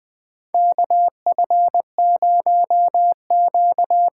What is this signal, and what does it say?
Morse code, heard as a keyed tone, about 20 wpm, 700 Hz. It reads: KF0Q
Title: USA Call Signs - 5 Characters - 8x Character Spacing - 20wpm